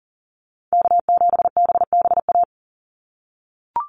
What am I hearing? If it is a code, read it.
Morse code, heard as a keyed tone, about 40 wpm, 700 Hz. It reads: K7BBA
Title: USA Call Signs 40wpm